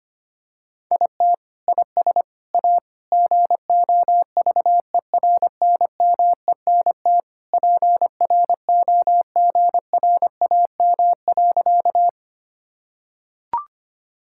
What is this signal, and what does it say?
Morse code, heard as a keyed tone, about 25 wpm, 700 Hz. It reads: IT IS A GOVERNMENT PROGRAM.
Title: Sentences from Top 1000 Words - Review Part 5 - 25wpm